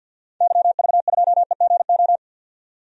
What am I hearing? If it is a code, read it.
Morse code, heard as a keyed tone, about 53 wpm, 700 Hz. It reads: YV2EGO
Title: DXCC-100 Call Signs - 6 Characters - 50wpm